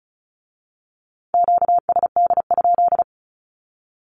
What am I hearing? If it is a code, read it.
Morse code, heard as a keyed tone, about 36 wpm, 700 Hz. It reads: QSD?
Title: Q-Codes with Spoken Meaning - 35wpm